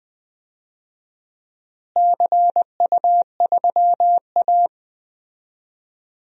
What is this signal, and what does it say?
Morse code, heard as a keyed tone, about 20 wpm, 700 Hz. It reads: CU3A
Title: DXCC-100 Call Signs - 4 Characters - Repeated 2x - 20wpm